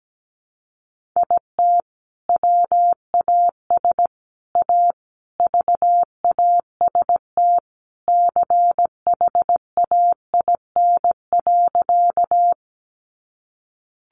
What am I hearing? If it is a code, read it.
Morse code, heard as a keyed tone, about 17 wpm, 700 Hz. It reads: IT WAS A VAST CHAIN.
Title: Sentences from Top 1500 Words 17wpm